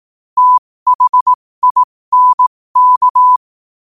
Morse code, heard as a keyed tone, about 18 wpm, spaced as for 15 wpm, 1000 Hz. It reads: THINK
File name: qotdfiles / 15wpm